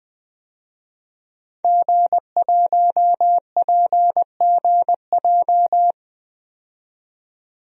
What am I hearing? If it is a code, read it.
Morse code, heard as a keyed tone, about 20 wpm, 700 Hz. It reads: G1PGJ
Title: DXCC-100 Call Signs - 5 Characters - 20wpm